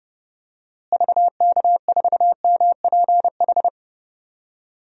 Morse code, heard as a keyed tone, about 30 wpm, 700 Hz. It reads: VK4MPH
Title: DXCC-100 Call Signs - 6 Characters - Repeated 3x - 30wpm